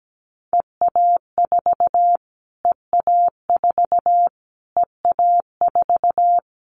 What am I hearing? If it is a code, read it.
Morse code, heard as a keyed tone, about 17 wpm, 700 Hz. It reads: EA4 EA4 EA4
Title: Callsign Prefixes for Top 25 DXCC - Repeated 3x - 17wpm